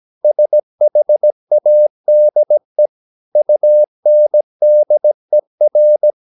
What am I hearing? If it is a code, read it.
Morse code, heard as a keyed tone, about 17 wpm, 600 Hz. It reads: SHADE UNDER
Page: Romanen Never af Ken Follet på engelsk.